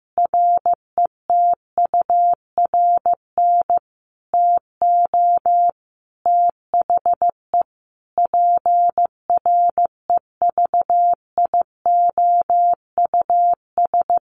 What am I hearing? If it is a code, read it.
Morse code, heard as a keyed tone, about 15 wpm, 700 Hz. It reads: RETURN TO THE PREVIOUS
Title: Sentences from Top 1500 Words 15wpm